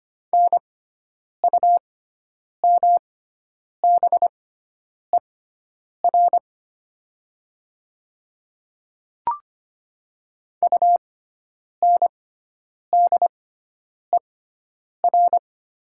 Morse code, heard as a keyed tone, about 25 wpm, spaced as for 9 wpm, 700 Hz. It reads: NUMBER UNDER
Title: 6x Character Spacing - Top 300 Words - 25wpm